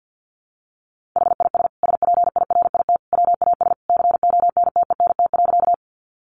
Morse code, heard as a keyed tone, about 50 wpm, 700 Hz. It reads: HIS SPIRIT WAS CONTENT.
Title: Sentences from Top 1400 Words 50wpm